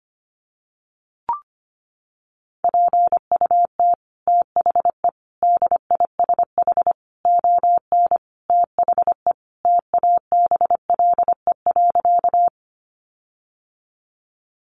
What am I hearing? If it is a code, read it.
Morse code, heard as a keyed tone, about 25 wpm, 700 Hz. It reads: PUT THE DISH ON THE TABLE.
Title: Sentences from Top 1300 Words 25wpm